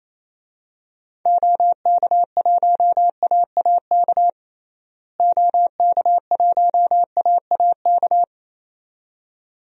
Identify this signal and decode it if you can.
Morse code, heard as a keyed tone, about 28 wpm, 700 Hz. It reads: OK1AAK OK1AAK
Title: DXCC-100 Call Signs - 6 Characters - Repeated 2x - 28wpm